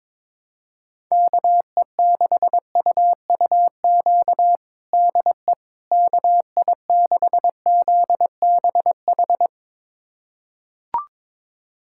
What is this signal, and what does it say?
Morse code, heard as a keyed tone, about 21 wpm, 700 Hz. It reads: KE6UUQ DE KI6ZBH
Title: USA Callsign DE Callsign 22wpm